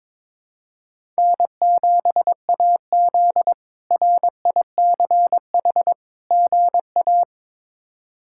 Morse code, heard as a keyed tone, about 22 wpm, 700 Hz. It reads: N7AZ RICH GA
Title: K1USN SST QSO Practice - 22wpm